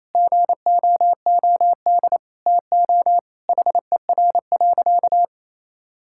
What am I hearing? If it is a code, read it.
Morse code, heard as a keyed tone, about 28 wpm, 700 Hz. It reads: GOOD TO HER.
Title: Sentences from Top 1000 Words - Review Part 5 - 28wpm